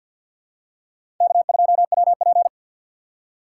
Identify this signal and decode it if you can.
Morse code, heard as a keyed tone, about 52 wpm, 700 Hz. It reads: K2WP
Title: USA Call Signs - 4 Characters - 2x Character Spacing - 50wpm